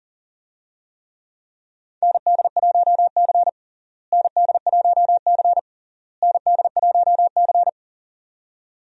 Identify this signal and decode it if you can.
Morse code, heard as a keyed tone, about 40 wpm, 700 Hz. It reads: ND1C ND1C ND1C
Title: USA Call Signs - 4 Characters - Repeated 3x - 40wpm